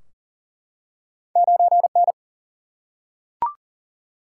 Morse code, heard as a keyed tone, about 40 wpm, 700 Hz. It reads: 9 N
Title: Single-Letter Single-Number 40wpm